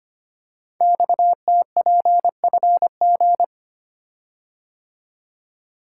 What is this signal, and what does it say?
Morse code, heard as a keyed tone, about 25 wpm, 700 Hz. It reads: XTPFG
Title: Code Groups - 5 Letters - 25wpm